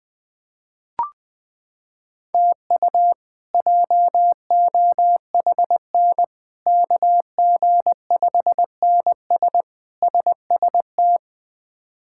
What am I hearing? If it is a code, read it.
Morse code, heard as a keyed tone, about 21 wpm, 700 Hz. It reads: TU JOHN KG5NS SST
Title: K1USN SST QSO Practice - 20wpm